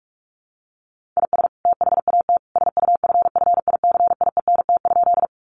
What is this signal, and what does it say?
Morse code, heard as a keyed tone, about 44 wpm, 700 Hz. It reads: IS THAT SUFFICIENT?
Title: Sentences from Top 1400 Words 45wpm